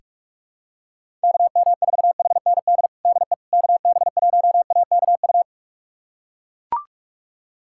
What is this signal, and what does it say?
Morse code, heard as a keyed tone, about 42 wpm, 700 Hz. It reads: KM4HND DE KB1AKU
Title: USA Callsign DE Callsign 45wpm